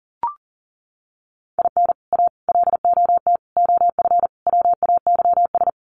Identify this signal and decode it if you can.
Morse code, heard as a keyed tone, about 40 wpm, 700 Hz. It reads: IN A LOT OF WAYS
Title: Sets of 5 Words Encore 40wpm